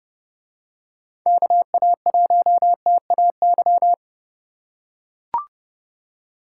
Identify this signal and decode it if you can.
Morse code, heard as a keyed tone, about 30 wpm, 700 Hz. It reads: KA1TAY